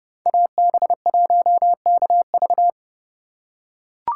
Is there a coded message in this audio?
Yes: Morse code from 0.3 to 2.7 s, heard as a keyed tone, about 30 wpm, 700 Hz. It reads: AB1KV